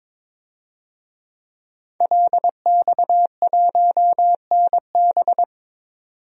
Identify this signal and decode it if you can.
Morse code, heard as a keyed tone, about 22 wpm, 700 Hz. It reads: LX1NB
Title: DXCC-100 Call Signs - 5 Characters - 22wpm